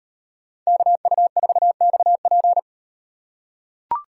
Morse code, heard as a keyed tone, about 38 wpm, 700 Hz. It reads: KU4XP